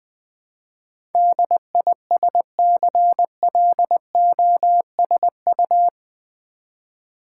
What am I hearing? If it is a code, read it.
Morse code, heard as a keyed tone, about 20 wpm, 700 Hz. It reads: DISCLOSU
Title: Build A Word - 4 Letters At A Time - 10-Letter Words - 20wpm